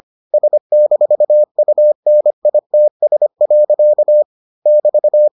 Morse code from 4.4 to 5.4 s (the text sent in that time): <BT>